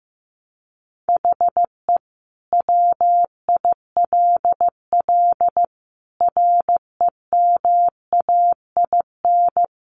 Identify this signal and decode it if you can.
Morse code, heard as a keyed tone, about 15 wpm, 700 Hz. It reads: HE WILL REMAIN
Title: Sentences from Top 1200 Words 15wpm